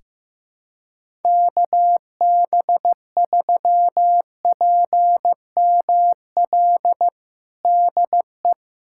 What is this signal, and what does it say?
Morse code, heard as a keyed tone, about 15 wpm, 700 Hz. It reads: KB3PML DE
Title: USA Callsign DE Callsign - 6 Characters - 15wpm